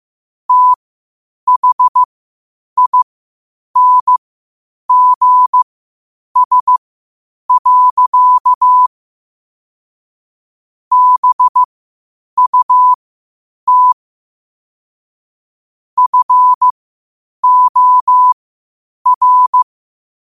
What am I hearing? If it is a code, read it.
Morse code, heard as a keyed tone, about 15 wpm, spaced as for 8 wpm, 1000 Hz. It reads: THINGS. BUT FOR
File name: qotdfiles / 8wpm